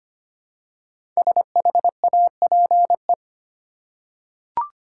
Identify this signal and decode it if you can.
Morse code, heard as a keyed tone, about 25 wpm, 700 Hz. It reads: SHAPE
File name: 5 Letter Words - First 3 Letters Given - 25wpm